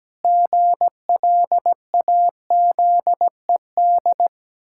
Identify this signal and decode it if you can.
Morse code, heard as a keyed tone, about 17 wpm, 700 Hz. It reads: GLAZED
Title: Pangram Challenge v2 - 17wpm